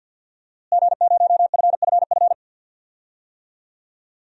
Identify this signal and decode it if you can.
Morse code, heard as a keyed tone, about 50 wpm, 700 Hz. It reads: G0FFF
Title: DXCC-100 Call Signs - 5 Characters - Repeated 3x - 50wpm